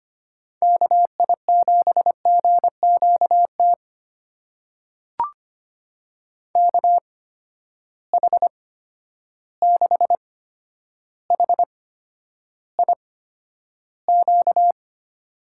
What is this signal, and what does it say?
Morse code, heard as a keyed tone, about 25 wpm, spaced as for 11 wpm, 700 Hz. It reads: KI7GQT KH6HIQ
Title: USA Call Signs - 6 Characters - 8x Character Spacing - 25wpm